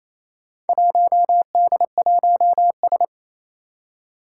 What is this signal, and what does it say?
Morse code, heard as a keyed tone, about 28 wpm, 700 Hz. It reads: 1D1S
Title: Four Challenging Letter-Number Combinations 28wpm